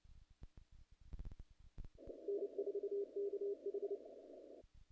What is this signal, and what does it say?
Morse code, heard as a keyed tone, about 29 wpm, 400 Hz. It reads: N4KH
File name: contests / naqp-cw-k5tr-op-k5pi- -1800